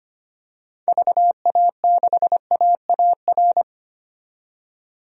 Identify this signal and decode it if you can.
Morse code, heard as a keyed tone, about 25 wpm, 700 Hz. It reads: VA6AAR
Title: DXCC-100 Call Signs - 6 Characters - 25wpm